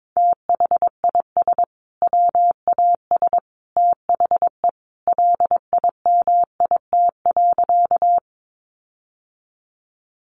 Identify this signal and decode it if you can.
Morse code, heard as a keyed tone, about 22 wpm, 700 Hz. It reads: THIS WAS THE LIMIT.
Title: Sentences from Top 1500 Words 22wpm